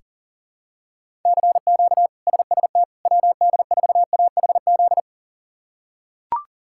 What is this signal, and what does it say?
Morse code, heard as a keyed tone, about 40 wpm, 700 Hz. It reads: CQ SST WD4AHZ
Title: K1USN SST QSO Practice - 40wpm